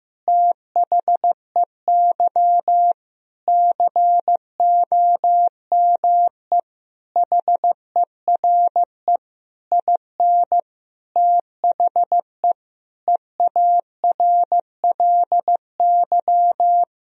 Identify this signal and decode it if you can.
Morse code, heard as a keyed tone, about 16 wpm, 700 Hz. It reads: THEY COME HERE IN THE EARLY
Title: Sentences from Top 1000 Words - Review Part 3 - 15wpm